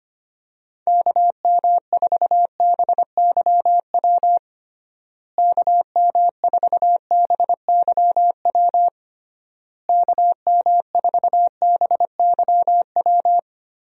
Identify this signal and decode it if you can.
Morse code, heard as a keyed tone, about 25 wpm, 700 Hz. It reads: KM4BYW KM4BYW KM4BYW
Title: USA Call Signs - 6 Characters - Repeated 3x - 25wpm